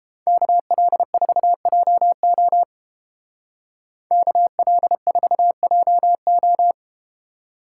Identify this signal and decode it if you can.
Morse code, heard as a keyed tone, about 30 wpm, 700 Hz. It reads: KL4JO KL4JO